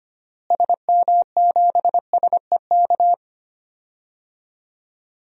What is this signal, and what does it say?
Morse code, heard as a keyed tone, about 25 wpm, 700 Hz. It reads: SM7SEK
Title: DXCC-100 Call Signs - 6 Characters - Repeated 3x - 25wpm